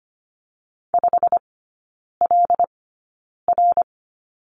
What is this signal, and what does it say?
Morse code, heard as a keyed tone, about 25 wpm, spaced as for 9 wpm, 700 Hz. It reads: 5LR
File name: course / 25:10wpm / 093 Lesson 16e - Letter c - Three Characters 25wpm